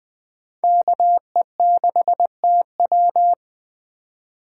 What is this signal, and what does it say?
Morse code, heard as a keyed tone, about 20 wpm, 700 Hz. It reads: KE6TW
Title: Speed Racing - USA 2x2 Call Signs 17wpm